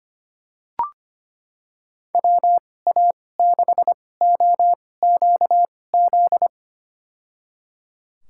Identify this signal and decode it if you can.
Morse code, heard as a keyed tone, about 25 wpm, spaced as for 18 wpm, 700 Hz. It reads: WA6OQZ
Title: USA Call Signs - 2x Character Spacing - 25wpm